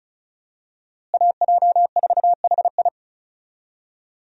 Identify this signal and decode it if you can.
Morse code, heard as a keyed tone, about 36 wpm, 700 Hz. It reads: AJ4HI
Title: USA Call Signs - 5 Characters - Repeated 3x - 35wpm